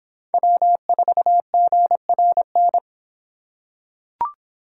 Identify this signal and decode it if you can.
Morse code, heard as a keyed tone, about 25 wpm, 700 Hz. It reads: W4GRN